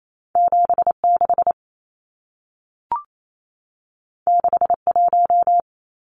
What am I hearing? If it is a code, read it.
Morse code, heard as a keyed tone, about 28 wpm, 700 Hz. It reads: 76 61
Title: Numbers - 2-Digit 28wpm